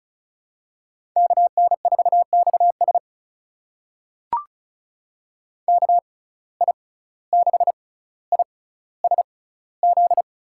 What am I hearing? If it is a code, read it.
Morse code, heard as a keyed tone, about 36 wpm, spaced as for 17 wpm, 700 Hz. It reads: KN4XS KI6ISZ
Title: USA Call Signs - 6x Character Spacing - 35wpm